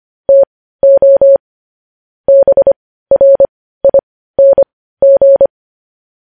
Morse code, heard as a keyed tone, about 25 wpm, spaced as for 15 wpm, 550 Hz. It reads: TO BRING
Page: news headlines in morse code at 15 wpm